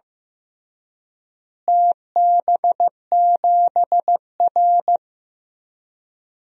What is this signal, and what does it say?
Morse code, heard as a keyed tone, about 15 wpm, 700 Hz. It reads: TB7R
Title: DXCC-100 Call Signs - 4 Characters - Repeated 3x - 15wpm